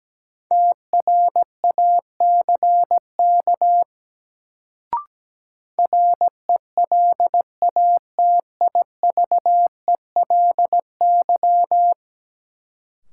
Morse code, heard as a keyed tone, about 17 wpm, 700 Hz. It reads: TRACK RELATIVELY
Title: Sentences from Top 1100 Words - New Word Drill - 17wpm